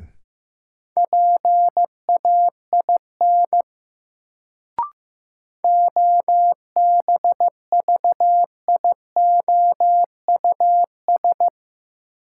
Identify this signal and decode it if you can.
Morse code, heard as a keyed tone, about 15 wpm, 700 Hz. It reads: PAIN OBVIOUS
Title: Sentences from Top 1600 Words - New Word Drill - 15wpm